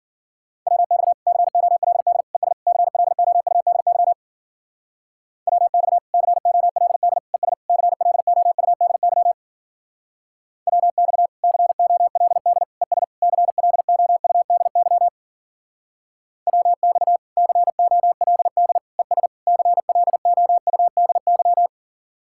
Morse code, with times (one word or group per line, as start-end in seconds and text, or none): 0.7-1.1 WX
1.3-2.2 COLD
2.4-2.5 ES
2.7-4.1 CLOUDY
5.5-6.0 WX
6.1-7.2 COLD
7.3-7.5 ES
7.7-9.3 CLOUDY
10.7-11.3 WX
11.4-12.6 COLD
12.8-13.0 ES
13.2-15.1 CLOUDY
16.5-17.2 WX
17.4-18.8 COLD
19.0-19.3 ES
19.5-21.7 CLOUDY